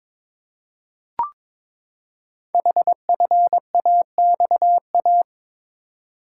Morse code, heard as a keyed tone, about 22 wpm, 700 Hz. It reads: HFAXA
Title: Code Groups - 5 Letters - 22wpm